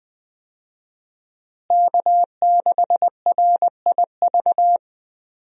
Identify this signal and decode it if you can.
Morse code, heard as a keyed tone, about 20 wpm, 700 Hz. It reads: K6RIV
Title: USA Call Signs - 5 Characters - Repeated 3x - 20wpm